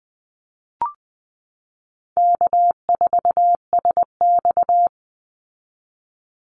Morse code, heard as a keyed tone, about 20 wpm, 700 Hz. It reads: K4SX
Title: Easy Call Signs Repeated 2x - 20wpm